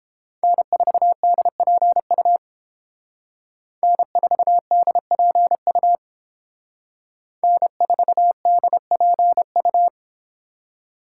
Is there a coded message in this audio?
Yes: Morse code from 0.4 to 9.9 s, heard as a keyed tone, about 29 wpm, 700 Hz. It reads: N4DPU N4DPU N4DPU